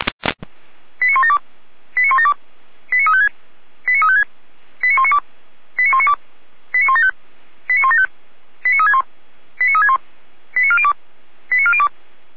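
Five-tone selective call (ZVEI), heard as quick runs of five tones, tones of about 80 ms. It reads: 89172 89172 89356 89356 89192 89192 89175 89175 89361 89361 89402 89402
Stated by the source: ZVEI five-tone call 89409 received by SDR.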